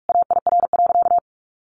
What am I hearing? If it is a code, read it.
Morse code, heard as a keyed tone, about 44 wpm, 700 Hz. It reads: AIR.
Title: Sentences from Top 400 Words 45wpm